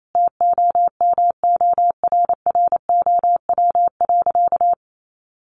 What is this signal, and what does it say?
Morse code, heard as a keyed tone, about 28 wpm, 700 Hz. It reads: TOMORROW.
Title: Sentences from Top 1100 Words 28wpm